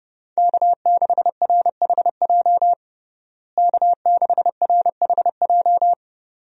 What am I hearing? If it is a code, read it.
Morse code, heard as a keyed tone, about 30 wpm, 700 Hz. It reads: K6RHJ K6RHJ
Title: USA Call Signs - 5 Characters - Repeated 2x - 30wpm